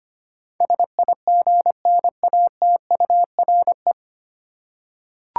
Morse code, heard as a keyed tone, about 25 wpm, 700 Hz. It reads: SIGNATURE
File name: Nine Letter Words - 25wpm